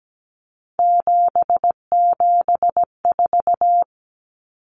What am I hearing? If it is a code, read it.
Morse code, heard as a keyed tone, about 17 wpm, 700 Hz. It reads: 774
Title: Common Cut Numbers - 3-Digit 17wpm